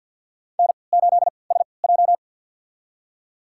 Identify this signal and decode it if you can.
Morse code, heard as a keyed tone, about 54 wpm, spaced as for 29 wpm, 700 Hz. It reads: N8SJ
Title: USA Call Signs - 3x Character Spacing - 50wpm